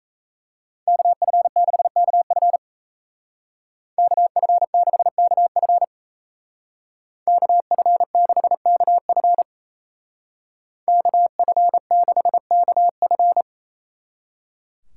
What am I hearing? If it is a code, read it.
Morse code, heard as a keyed tone, about 35 wpm, 700 Hz. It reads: KF6KF KF6KF KF6KF KF6KF
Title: Speed Racing - USA 2x2 Call Signs 28wpm